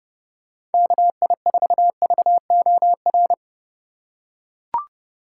Morse code, heard as a keyed tone, about 32 wpm, 700 Hz. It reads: KI4VOR
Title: USA Call Signs - 3x Character Spacing - 30wpm